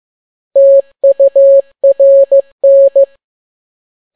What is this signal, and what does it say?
Morse code, heard as a keyed tone, about 15 wpm, 550 Hz. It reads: TURN
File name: n6tt p1 15C13